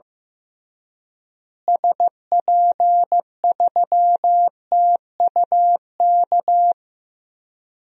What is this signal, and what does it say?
Morse code, heard as a keyed tone, about 15 wpm, 700 Hz. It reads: SP3TUK
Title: DXCC-100 Call Signs - 6 Characters - Repeated 3x - 15wpm